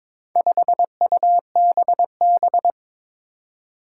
Morse code, heard as a keyed tone, about 22 wpm, 700 Hz. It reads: 5UBB
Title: Four Challenging Letter-Number Combinations 22wpm